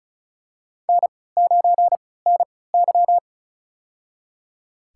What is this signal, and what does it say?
Morse code, heard as a keyed tone, about 34 wpm, spaced as for 20 wpm, 700 Hz. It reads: N9NY
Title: USA Call Signs - 3x Character Spacing - 35wpm